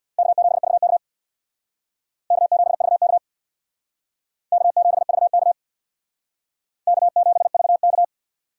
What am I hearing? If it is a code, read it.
Morse code, heard as a keyed tone, about 62 wpm, 700 Hz. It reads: K7VK K7VK K7VK K7VK
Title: Speed Racing - USA 1x2 Call Signs 50wpm